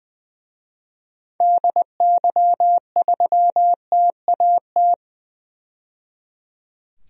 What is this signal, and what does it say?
Morse code, heard as a keyed tone, about 20 wpm, 700 Hz. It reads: DY3TAT
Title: DXCC-100 Call Signs - 6 Characters - 20wpm